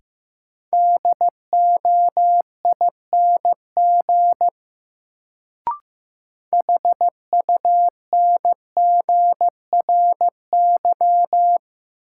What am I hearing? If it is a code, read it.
Morse code, heard as a keyed tone, about 15 wpm, 700 Hz. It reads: DOING HUNGRY
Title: Sentences from Top 1400 Words - New Word Drill - 15wpm